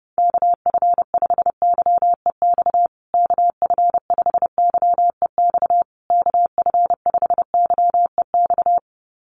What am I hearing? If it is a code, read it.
Morse code, heard as a keyed tone, about 30 wpm, 700 Hz. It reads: KF5YEX KF5YEX KF5YEX
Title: USA Call Signs Repeated 3x 30wpm